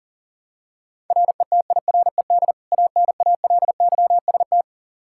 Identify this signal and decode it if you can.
Morse code, heard as a keyed tone, about 40 wpm, 700 Hz. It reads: RETIRED ANALYST
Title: Practice QSO - Exchange 3 - More Info - 40wpm